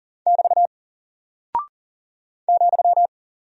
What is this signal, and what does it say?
Morse code, heard as a keyed tone, about 40 wpm, 700 Hz. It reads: <BT> ,
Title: Single Character (Letter:Number:Punctuation:Prosign) Alt 40wpm